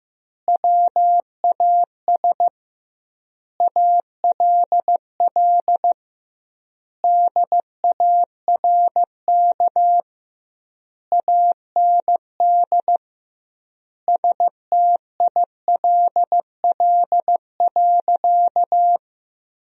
Morse code, heard as a keyed tone, about 15 wpm, 700 Hz. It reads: WAS ALL DARK AND STILL.
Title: 2x Word-Spacing - Sentences from Top 500 Words - 15wpm